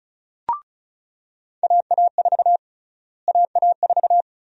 Morse code, heard as a keyed tone, about 36 wpm, 700 Hz. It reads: AA4 AA4
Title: USA Call Sign Prefixes - Repeated 2x - 35wpm